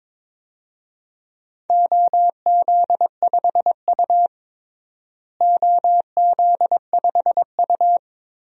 Morse code, heard as a keyed tone, about 22 wpm, 700 Hz. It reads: OZ5U OZ5U
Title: DXCC-100 Call Signs - 4 Characters - Repeated 3x - 22wpm